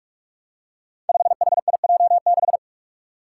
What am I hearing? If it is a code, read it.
Morse code, heard as a keyed tone, about 42 wpm, 700 Hz. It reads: 5HIJ6
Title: Five Challenging Letter-Number Combinations 45wpm